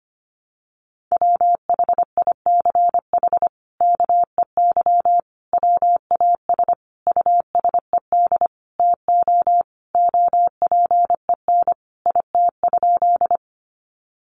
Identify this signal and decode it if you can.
Morse code, heard as a keyed tone, about 25 wpm, 700 Hz. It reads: WHICH KEY WAS USED TO OPEN IT?
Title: Sentences from Top 600 Words 25wpm